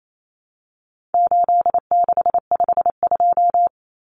Morse code, heard as a keyed tone, about 28 wpm, 700 Hz. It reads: 8652
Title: Common Cut Numbers - 4-Digit 28wpm